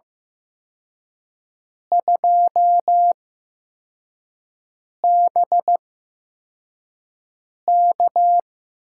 Morse code, heard as a keyed tone, about 15 wpm, spaced as for 4 wpm, 700 Hz. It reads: 2BK
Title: USA Call Signs - 8x Character Spacing - 15wpm